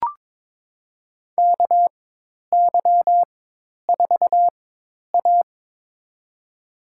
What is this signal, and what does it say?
Morse code, heard as a keyed tone, about 21 wpm, spaced as for 10 wpm, 700 Hz. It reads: KY4A
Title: USA Call Signs - 4 Characters - 4x Character Spacing - 22wpm